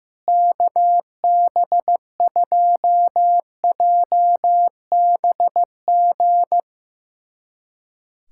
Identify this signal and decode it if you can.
Morse code, heard as a keyed tone, about 15 wpm, 700 Hz. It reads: KB2JBG
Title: USA Callsign DE Callsign - 6 Characters - 15wpm